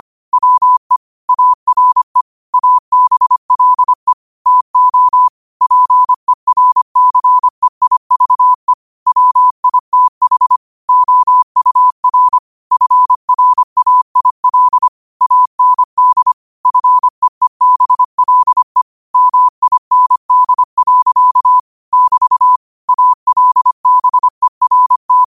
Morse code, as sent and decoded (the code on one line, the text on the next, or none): .-- . / .- .-. / .- -... .-.. / - --- / .--. .-. -.-. .. ...- . / .-- .. - .... / --- ..- .-. / ..-. .-. .- .. .-.. / .- -. -.. / ..-. . -... .-.. / -- .. -. -.. .-.-.- / -...- / .- .-.. -... .-. -
WE ARE ABLE TO PERCEIVE WITH OUR FRAIL AND FEEBLE MIND. <BT> ALBERT